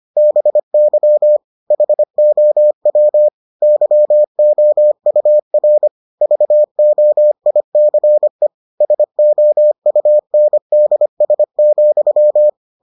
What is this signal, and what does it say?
Morse code, heard as a keyed tone, about 25 wpm, 600 Hz. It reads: BY HOW YOUR VOICE SOUNDS,